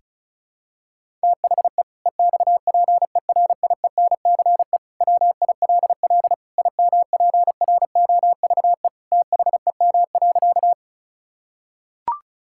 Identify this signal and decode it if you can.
Morse code, heard as a keyed tone, about 36 wpm, 700 Hz. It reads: THE EXPERIENCE WILL IMPROVE THEM.
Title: Sentences from Top 1000 Words - Review Part 5 - 35wpm